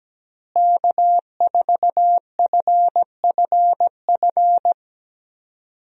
Morse code, heard as a keyed tone, about 17 wpm, 700 Hz. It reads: K4FFF
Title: USA Call Signs - 5 Characters - Repeated 3x - 17wpm